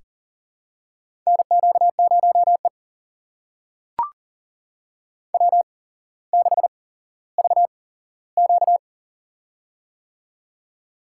Morse code, heard as a keyed tone, about 40 wpm, spaced as for 17 wpm, 700 Hz. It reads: NQ0E W6VQ
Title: USA Call Signs - 4 Characters - 8x Character Spacing - 40wpm